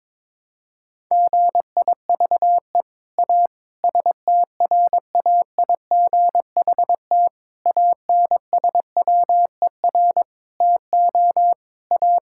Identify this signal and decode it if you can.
Morse code, heard as a keyed tone, about 22 wpm, 700 Hz. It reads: GIVE A STRAIGHT ANSWER TO A
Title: Sentences from Top 1000 Words - Review Part 6 - 22wpm